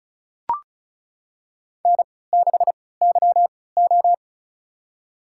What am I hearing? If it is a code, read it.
Morse code, heard as a keyed tone, about 36 wpm, spaced as for 20 wpm, 700 Hz. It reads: N6YO
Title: USA Call Signs - 4 Characters - 3x Character Spacing - 35wpm